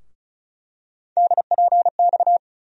Morse code, heard as a keyed tone, about 34 wpm, 700 Hz. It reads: DPX
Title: USA Call Sign Postfixes - 3 Letters 35wpm